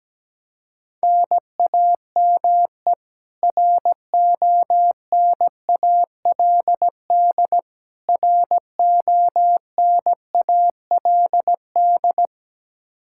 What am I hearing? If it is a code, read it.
Morse code, heard as a keyed tone, about 17 wpm, 700 Hz. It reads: NAME RONALD RONALD
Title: Practice QSO - Exchange 1 - Beginning - 17wpm